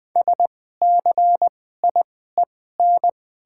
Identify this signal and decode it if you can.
Morse code, heard as a keyed tone, about 21 wpm, spaced as for 15 wpm, 700 Hz. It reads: SCIEN
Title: 2x Character Spacing - Top 500 Words - 20wpm